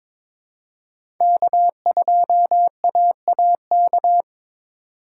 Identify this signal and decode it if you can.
Morse code, heard as a keyed tone, about 22 wpm, 700 Hz. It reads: K2AAK
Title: USA Call Signs - 5 Characters - 2x Character Spacing - 22wpm